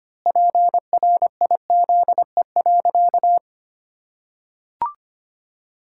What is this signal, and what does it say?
Morse code, heard as a keyed tone, about 25 wpm, 700 Hz. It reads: PRIZE.
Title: Pangram Challenge v2 - 25wpm